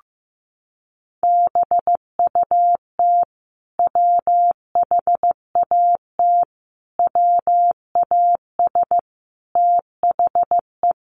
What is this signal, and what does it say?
Morse code, heard as a keyed tone, about 15 wpm, 700 Hz. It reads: BUT WHAT WAS THE
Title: Sentences from Top 1100 Words 15wpm